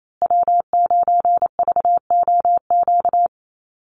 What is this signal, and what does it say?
Morse code, heard as a keyed tone, about 28 wpm, 700 Hz. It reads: W9VOQ
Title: USA Call Signs Repeated 3x 28wpm